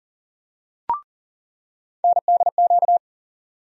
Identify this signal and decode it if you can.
Morse code, heard as a keyed tone, about 40 wpm, 700 Hz. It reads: NDQ
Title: USA Call Sign Postfixes - 3 Letters 40wpm